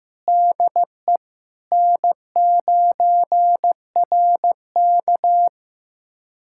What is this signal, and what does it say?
Morse code, heard as a keyed tone, about 15 wpm, 700 Hz. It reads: DE N9RK
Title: USA Callsign DE Callsign - 4 Characters - 15wpm